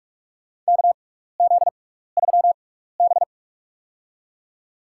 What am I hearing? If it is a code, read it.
Morse code, heard as a keyed tone, about 41 wpm, spaced as for 15 wpm, 700 Hz. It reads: KZ3B
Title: USA Call Signs - 4 Characters - 6x Character Spacing - 45wpm